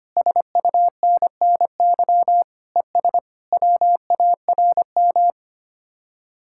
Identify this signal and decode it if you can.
Morse code, heard as a keyed tone, about 30 wpm, 700 Hz. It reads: SUNNY ES WARM
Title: Speed-Racing Common QSO Phrases Encore - 25wpm